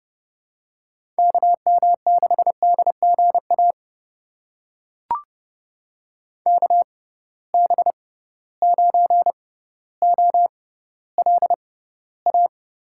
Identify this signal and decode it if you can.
Morse code, heard as a keyed tone, about 32 wpm, spaced as for 16 wpm, 700 Hz. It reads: KM6DGA KB9OLA
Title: USA Call Signs - 6 Characters - 6x Character Spacing - 30wpm